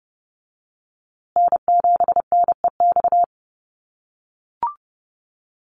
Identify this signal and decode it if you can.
Morse code, heard as a keyed tone, about 30 wpm, 700 Hz. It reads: N7NEX
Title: USA Call Signs Repeated 2x 30wpm